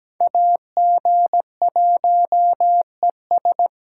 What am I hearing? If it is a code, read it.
Morse code, heard as a keyed tone, about 20 wpm, 700 Hz. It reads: AG1ES
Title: Speed Racing - USA 2x2 Call Signs 17wpm